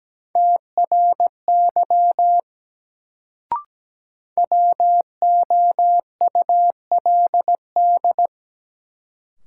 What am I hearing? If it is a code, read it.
Morse code, heard as a keyed tone, about 17 wpm, 700 Hz. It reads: TRY WOULD
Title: Sets of 2 Words - New Word Drill - 17wpm